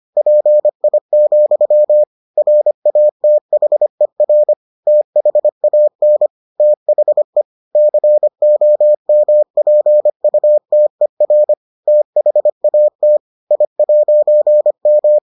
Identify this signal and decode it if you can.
Morse code, heard as a keyed tone, about 25 wpm, 600 Hz. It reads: PI, RATHER THAN THE COMPUTER THAT I'M